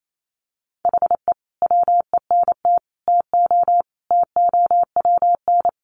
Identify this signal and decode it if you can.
Morse code, heard as a keyed tone, about 28 wpm, 700 Hz. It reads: HE WENT TO TOWN
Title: Sentences from Top 1300 Words 28wpm